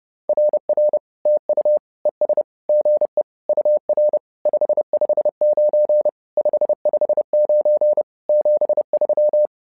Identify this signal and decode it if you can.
Morse code, heard as a keyed tone, about 30 wpm, 600 Hz. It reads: RR TU ES GE UR 559 559 73
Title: SOTA QSO Practice 30wpm